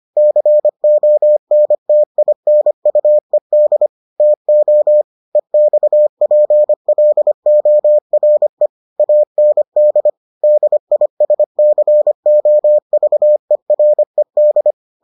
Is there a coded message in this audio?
Yes: Morse code from 0.2 to 14.7 s, heard as a keyed tone, about 25 wpm, 600 Hz. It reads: CONTINUED TO EXPLORE AND DISCOVERED